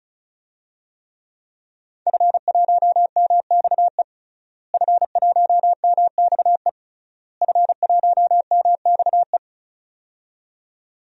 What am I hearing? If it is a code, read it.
Morse code, heard as a keyed tone, about 34 wpm, 700 Hz. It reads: F1MXE F1MXE F1MXE
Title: DXCC-100 Call Signs - 5 Characters - Repeated 3x - 35wpm